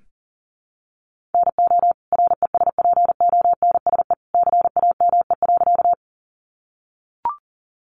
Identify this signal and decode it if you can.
Morse code, heard as a keyed tone, about 37 wpm, 700 Hz. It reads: NO RESPONSE CAME.
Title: Sentences from Top 1400 Words 40wpm